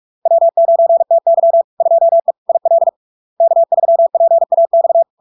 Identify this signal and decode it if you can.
Morse code, heard as a keyed tone, about 46 wpm, 670 Hz. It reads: W9TY 2E IL K3PAX
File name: LARK-FD-2022-Log-in-CW-45wpm-670-Hz